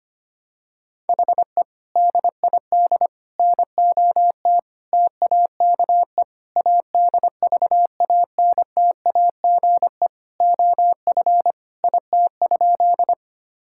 Morse code, heard as a keyed tone, about 25 wpm, 700 Hz. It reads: HE DID NOT TAKE ADVANTAGE OF IT?
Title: Sentences from Top 1000 Words - Review Part 6 - 25wpm